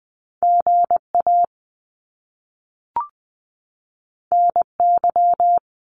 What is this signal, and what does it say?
Morse code, heard as a keyed tone, about 20 wpm, 700 Hz. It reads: GA NY
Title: US States 20wpm